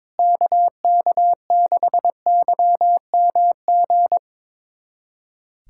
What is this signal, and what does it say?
Morse code, heard as a keyed tone, about 22 wpm, 700 Hz. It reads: KK6YMG
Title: USA Callsign DE Callsign 22wpm